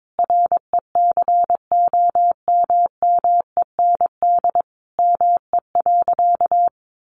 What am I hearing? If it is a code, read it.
Morse code, heard as a keyed tone, about 22 wpm, 700 Hz. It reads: RECOMMEND ME.
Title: Sentences from Top 1200 Words 22wpm